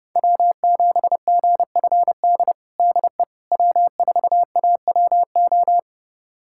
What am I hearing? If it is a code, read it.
Morse code, heard as a keyed tone, about 30 wpm, 700 Hz. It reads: W7GFD DE W4AWO
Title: USA Callsign DE Callsign - 5 Characters - 30wpm